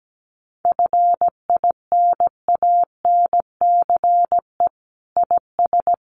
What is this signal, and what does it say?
Morse code, heard as a keyed tone, about 17 wpm, 700 Hz. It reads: FINANCE IS
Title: Sentences from Top 1500 Words 17wpm